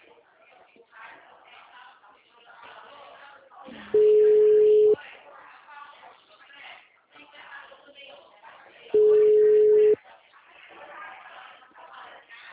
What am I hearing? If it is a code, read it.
Ringback tone (Europe, 425 Hz), with speech.